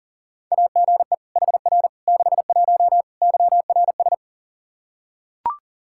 Morse code, heard as a keyed tone, about 37 wpm, 700 Hz. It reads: AGE HR 61 YRS
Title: Practice QSO - Exchange 3 - More Info - 40wpm